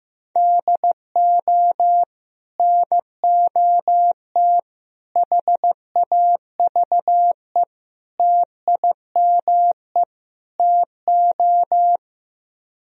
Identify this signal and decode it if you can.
Morse code, heard as a keyed tone, about 15 wpm, 700 Hz. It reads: DO NOT HAVE TIME TO
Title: Speed-Racing - Sets of 5 Words 15wpm